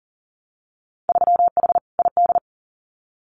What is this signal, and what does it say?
Morse code, heard as a keyed tone, about 38 wpm, 700 Hz. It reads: 3H ID
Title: Winter Field Day Exchanges Encore 40wpm